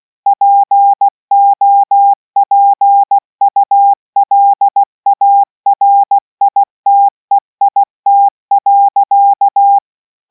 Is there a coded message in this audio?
Yes: Morse code from 0.3 to 9.8 s, heard as a keyed tone, about 16 wpm, 800 Hz. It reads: POPULARITEIT.